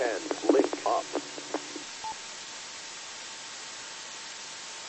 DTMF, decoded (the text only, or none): none